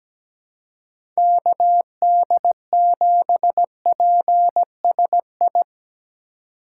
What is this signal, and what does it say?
Morse code, heard as a keyed tone, about 17 wpm, 700 Hz. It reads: KD7PSI
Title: USA Call Signs - 6 Characters - 3x Character Spacing - 17wpm